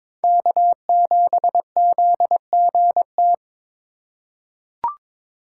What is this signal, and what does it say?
Morse code, heard as a keyed tone, about 22 wpm, 700 Hz. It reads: K7ZGT
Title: USA Call Signs - 5 Characters - 3x Character Spacing - 22wpm